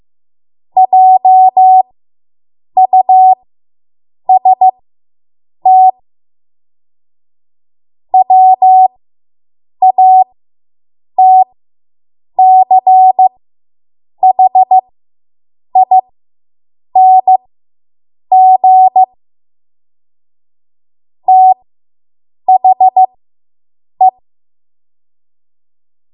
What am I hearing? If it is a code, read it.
Morse code, heard as a keyed tone, about 15 wpm, spaced as for 7 wpm, 750 Hz. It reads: JUST WATCHING THE